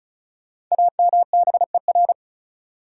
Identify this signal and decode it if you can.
Morse code, heard as a keyed tone, about 36 wpm, 700 Hz. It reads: AMBER
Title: USA Common Names - Max 5 Letters - 35wpm